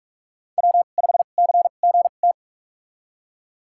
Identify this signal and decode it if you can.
Morse code, heard as a keyed tone, about 46 wpm, spaced as for 32 wpm, 700 Hz. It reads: W5CGT
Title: USA Call Signs - 2x Character Spacing - 45wpm